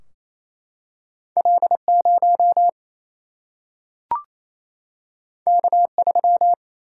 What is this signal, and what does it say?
Morse code, heard as a keyed tone, about 28 wpm, 700 Hz. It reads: L0 K3
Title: Single-Letter Single-Number 28wpm